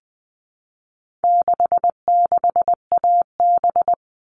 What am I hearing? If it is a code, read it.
Morse code, heard as a keyed tone, about 20 wpm, 700 Hz. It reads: 66AB
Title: All Cut Numbers - 4-Digit 20wpm